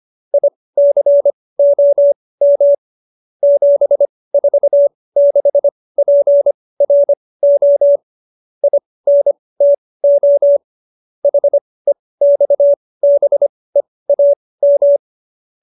Morse code, heard as a keyed tone, about 25 wpm, spaced as for 18 wpm, 570 Hz. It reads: ICOM 746PRO INTO HEXBEAM
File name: qso222 18